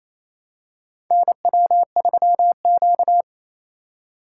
Morse code, heard as a keyed tone, about 28 wpm, 700 Hz. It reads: NW3Q
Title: USA Call Signs - 4 Characters - 2x Character Spacing - 28wpm